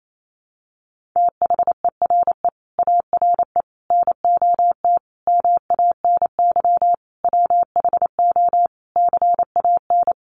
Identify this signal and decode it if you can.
Morse code, heard as a keyed tone, about 28 wpm, 700 Hz. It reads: THERE ARE NOT MANY WHO CAN
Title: Sentences from Top 1300 Words 28wpm